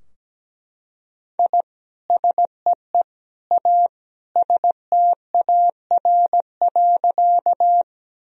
Morse code, heard as a keyed tone, about 17 wpm, 700 Hz. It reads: I SEE A STAR.
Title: Sentences from Top 1000 Words - Review Part 2 - 17wpm